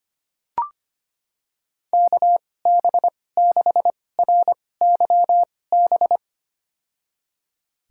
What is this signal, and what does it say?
Morse code, heard as a keyed tone, about 25 wpm, spaced as for 18 wpm, 700 Hz. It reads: KB6RYB
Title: USA Call Signs - 6 Characters - 2x Character Spacing - 25wpm